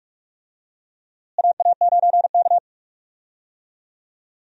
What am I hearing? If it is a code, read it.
Morse code, heard as a keyed tone, about 45 wpm, 700 Hz. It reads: AA9K
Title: USA Call Signs - 4 Characters - Repeated 2x - 45wpm